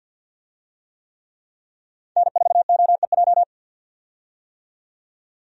Morse code, heard as a keyed tone, about 51 wpm, 700 Hz. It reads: N4OEJ
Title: USA Call Signs - 5 Characters - 50wpm